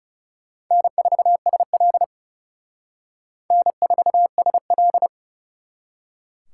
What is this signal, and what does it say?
Morse code, heard as a keyed tone, about 35 wpm, 700 Hz. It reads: N4SL N4SL
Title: Speed Racing - USA 1x2 Call Signs 30wpm